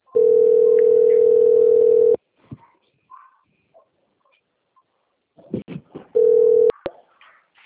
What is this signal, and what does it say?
Ringback tone (North America), 440 + 480 Hz.
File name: out- - - - - .14143